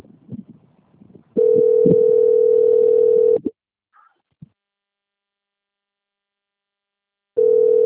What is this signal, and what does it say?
Ringback tone (North America), 440 + 480 Hz.